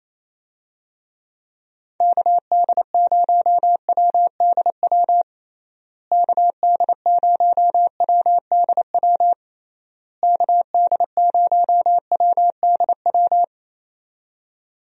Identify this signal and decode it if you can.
Morse code, heard as a keyed tone, about 28 wpm, 700 Hz. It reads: KD0WDW KD0WDW KD0WDW
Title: USA Call Signs - 6 Characters - Repeated 3x - 28wpm